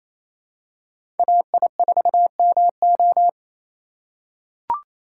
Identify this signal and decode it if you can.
Morse code, heard as a keyed tone, about 28 wpm, 700 Hz. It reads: AI4MO
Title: USA Call Signs - 5 Characters - 6x Character Spacing - 28wpm